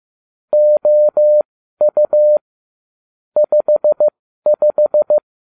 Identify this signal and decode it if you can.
Morse code, heard as a keyed tone, about 15 wpm, spaced as for 12 wpm, 600 Hz. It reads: OU 55